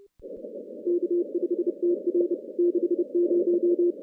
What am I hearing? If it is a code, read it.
Morse code, heard as a keyed tone, about 30 wpm, 350 Hz. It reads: K5TR 60
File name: contests / CW-SS-k5tr- -2200